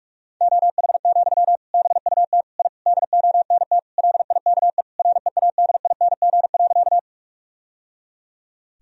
Morse code, heard as a keyed tone, about 46 wpm, 700 Hz. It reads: OH, BUT I DONT LIKE READING.
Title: Sentences from Top 1000 Words - Review Part 3 - 45wpm